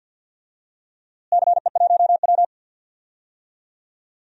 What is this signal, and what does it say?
Morse code, heard as a keyed tone, about 47 wpm, 700 Hz. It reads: CE1W
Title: DXCC-100 Call Signs - 4 Characters - Repeated 2x - 50wpm